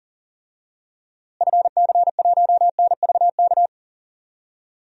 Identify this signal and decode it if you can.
Morse code, heard as a keyed tone, about 40 wpm, 700 Hz. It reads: FC1NVK
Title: DXCC-100 Call Signs - 6 Characters - Repeated 3x - 40wpm